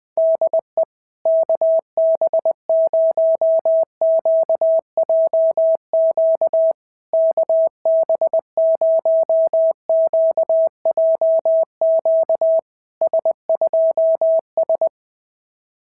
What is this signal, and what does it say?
Morse code, heard as a keyed tone, about 20 wpm, 650 Hz. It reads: DE KB0QJQ KB0QJQ S2S
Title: SOTA QSO Practice 20wpm